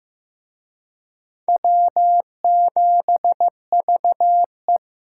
Morse code, heard as a keyed tone, about 18 wpm, 700 Hz. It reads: W7VE
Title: Speed Racing - USA 1x2 Call Signs 15wpm